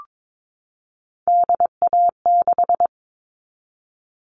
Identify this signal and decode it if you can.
Morse code, heard as a keyed tone, about 22 wpm, 700 Hz. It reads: DA6
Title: All Cut Numbers - 3-Digit 22wpm